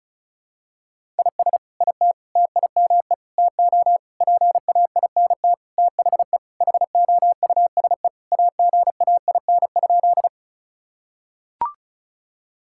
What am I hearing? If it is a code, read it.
Morse code, heard as a keyed tone, about 34 wpm, 700 Hz. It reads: IS IT TIME TO PAINT THE HOUSE AGAIN?
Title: Sentences from Top 1000 Words - Review Part 6 - 35wpm